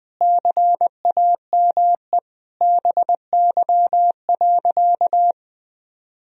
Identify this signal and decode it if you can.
Morse code, heard as a keyed tone, about 21 wpm, 700 Hz. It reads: CAME BY.
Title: Sentences from Top 1000 Words - Review Part 1 - 20wpm